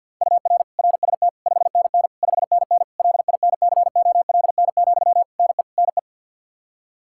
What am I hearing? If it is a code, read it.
Morse code, heard as a keyed tone, about 47 wpm, 700 Hz. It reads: UR RST 5NN 5NN LINCOLN, NE NE
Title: Practice QSO - Exchange 1 - Beginning - 50wpm